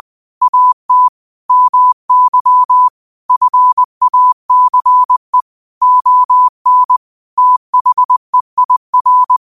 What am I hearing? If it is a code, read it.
Morse code, heard as a keyed tone, about 20 wpm, 1000 Hz. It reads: AT MY FACE ON THEIR